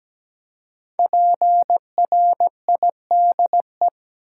Morse code, heard as a keyed tone, about 17 wpm, 700 Hz. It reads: PRIDE
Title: Five Letter Words - 17wpm